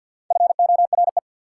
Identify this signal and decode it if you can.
Morse code, heard as a keyed tone, about 51 wpm, 690 Hz. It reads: FORE
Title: Prefixes and Suffixes 50wpm